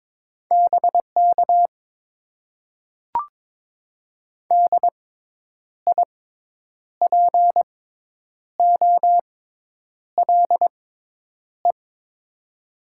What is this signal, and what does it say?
Morse code, heard as a keyed tone, about 22 wpm, spaced as for 9 wpm, 700 Hz. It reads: BK DIPOLE
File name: Top 90 QSO Elements - 6x Character Spacing - 22wpm